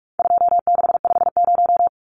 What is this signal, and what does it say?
Morse code, heard as a keyed tone, about 44 wpm, 700 Hz. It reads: 2650
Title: Numbers - 4-Digit 45wpm